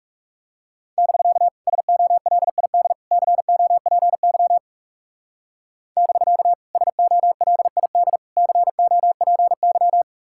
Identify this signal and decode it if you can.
Morse code, heard as a keyed tone, about 36 wpm, 700 Hz. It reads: <BK> SOLID COPY <BK> SOLID COPY
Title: Speed-Racing Common QSO Phrases Encore - 30wpm